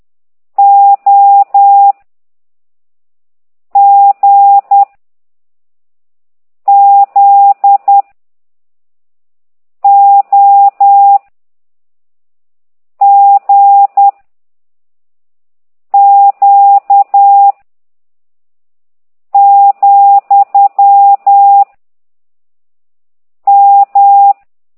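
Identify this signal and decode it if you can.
Morse code, heard as a keyed tone, about 10 wpm, 800 Hz. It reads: O G Z O G Q , M
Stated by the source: Morse code lesson 5 exercise 10 by CT2IDW, containing only characters M, O, G, Z, Q and comma.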